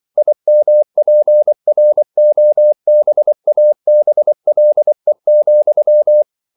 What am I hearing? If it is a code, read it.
Morse code, heard as a keyed tone, about 24 wpm, 600 Hz. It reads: IMPROBABLE,
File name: CW / Practice / EnglishQuotations0224